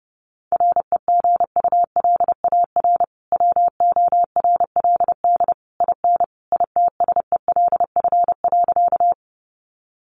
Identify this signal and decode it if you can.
Morse code, heard as a keyed tone, about 30 wpm, 700 Hz. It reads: REGULAR WORLD IN ITSELF.
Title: Sentences from Top 900 Words 30wpm